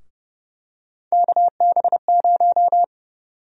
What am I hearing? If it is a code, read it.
Morse code, heard as a keyed tone, about 30 wpm, 700 Hz. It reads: KB0
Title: USA Call Sign Prefixes - Repeated 2x - 30wpm